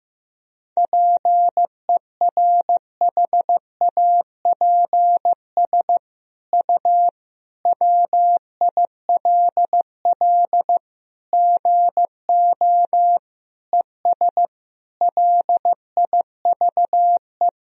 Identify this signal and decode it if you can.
Morse code, heard as a keyed tone, about 16 wpm, 700 Hz. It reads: PERHAPS U WILL GO ES LIVE
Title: Sentences from Top 500 Words and Abbreviations 15wpm